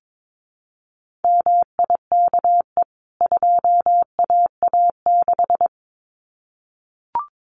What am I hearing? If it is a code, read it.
Morse code, heard as a keyed tone, about 22 wpm, 700 Hz. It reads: MIKE 2AA6
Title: CWT Exchange 22wpm